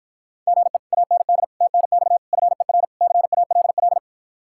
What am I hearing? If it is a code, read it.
Morse code, heard as a keyed tone, about 52 wpm, 700 Hz. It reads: GE AND TNX FER CALL